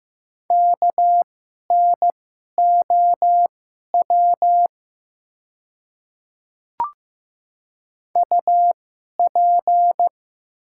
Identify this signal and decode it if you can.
Morse code, heard as a keyed tone, about 15 wpm, spaced as for 11 wpm, 700 Hz. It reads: KNOW UP
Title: 2x Character Spacing - Top 100 Words - 15wpm